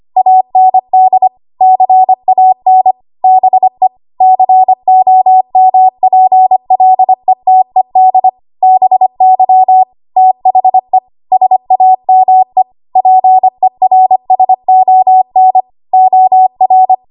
Morse code, heard as a keyed tone, about 25 wpm, 750 Hz. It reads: AND CAN BE COMPLETED BY THE SAME PERSON OR
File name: Morse / 25WPM